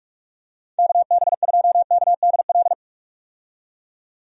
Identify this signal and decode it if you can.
Morse code, heard as a keyed tone, about 46 wpm, 700 Hz. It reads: KB2KDL
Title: USA Call Signs - 6 Characters - 45wpm